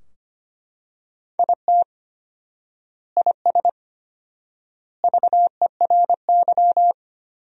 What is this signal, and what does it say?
Morse code, heard as a keyed tone, about 25 wpm, 700 Hz. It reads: IT IS VERY
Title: 4x Word-Spacing - Sentences from Top 200 Words - 25wpm